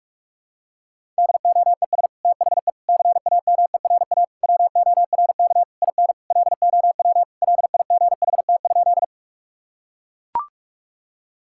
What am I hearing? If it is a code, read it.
Morse code, heard as a keyed tone, about 41 wpm, 700 Hz. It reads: DOES THE CAMERA WORK IN LOW LIGHT?
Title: Sentences from Top 1000 Words - Review Part 5 - 45wpm